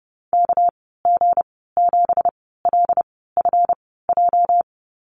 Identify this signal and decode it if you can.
Morse code, heard as a keyed tone, about 30 wpm, spaced as for 17 wpm, 700 Hz. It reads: KG7LFJ
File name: Lesson 37g - Letter x - Call Signs 30wpm